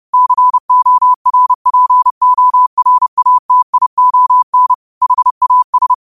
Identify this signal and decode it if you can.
Morse code, heard as a keyed tone, about 30 wpm, 1000 Hz. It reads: CORPORATION HAS
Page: Quotes for Fri, 15 Aug 2025 in Morse Code at 30 words per minute.